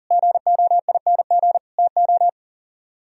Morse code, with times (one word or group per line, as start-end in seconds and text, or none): 0.1-1.6 GOING
1.8-2.3 TO